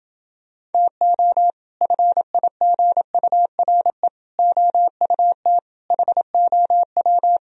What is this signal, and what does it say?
Morse code, heard as a keyed tone, about 28 wpm, 700 Hz. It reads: TO FIGURE OUT HOW